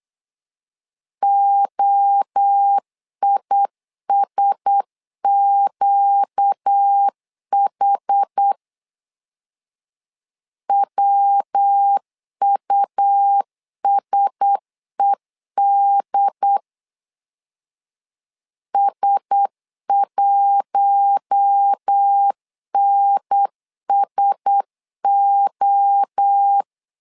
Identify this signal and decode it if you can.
Morse code, heard as a keyed tone, about 8 wpm, 800 Hz. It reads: OISQH WUSED S1NSO
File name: LEZIONI / 4 Q1S 8wpm 2710